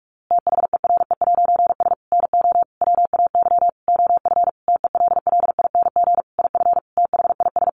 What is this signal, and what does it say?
Morse code, heard as a keyed tone, about 44 wpm, 700 Hz. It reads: THERE'S NO WAY OF TELLING IF THIS
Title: A Sound Of Thunder 45wpm